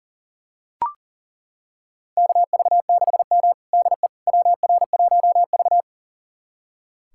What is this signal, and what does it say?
Morse code, heard as a keyed tone, about 37 wpm, 700 Hz. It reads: KV6M DE WR1V
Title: USA Callsign DE Callsign - 4 Characters - 40wpm